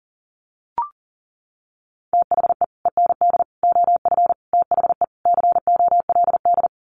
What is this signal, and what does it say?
Morse code, heard as a keyed tone, about 40 wpm, 700 Hz. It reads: THE END OF THE COLD
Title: Sets of 5 Words 40wpm